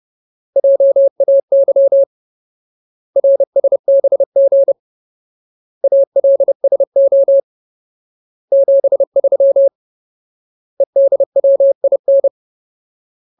Morse code, heard as a keyed tone, about 30 wpm, 550 Hz. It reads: JAY RSBG ALSO 73 EDWIN
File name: Ultimate Word List / 30WPM Ultimate